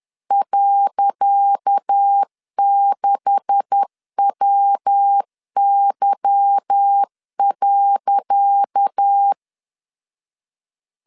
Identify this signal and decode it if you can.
Morse code, heard as a keyed tone, about 10 wpm, 800 Hz. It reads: .6WY.
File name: LEZIONI / 10 SIGNS / 10C DOT 10wpm 2940